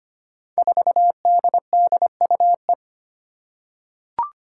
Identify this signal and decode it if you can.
Morse code, heard as a keyed tone, about 25 wpm, 700 Hz. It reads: 4DDUE
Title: Five Challenging Letter-Number Combinations 25wpm